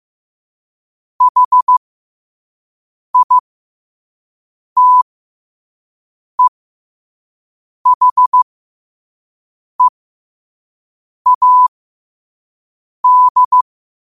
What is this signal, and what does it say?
Morse code, heard as a keyed tone, about 15 wpm, spaced as for 5 wpm, 1000 Hz. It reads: HITEHEAD
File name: qotdfiles / 5wpm